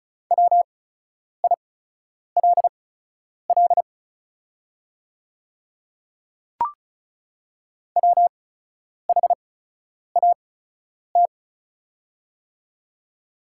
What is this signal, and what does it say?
Morse code, heard as a keyed tone, about 36 wpm, spaced as for 10 wpm, 690 Hz. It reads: WILL WHAT
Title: 8x Character Spacing - Top 100 Words - 35wpm